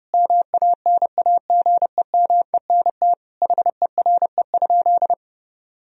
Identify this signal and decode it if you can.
Morse code, heard as a keyed tone, about 30 wpm, 700 Hz. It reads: MANAGEMENT HERE?
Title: Sentences from Top 1000 Words - Review Part 4 - 30wpm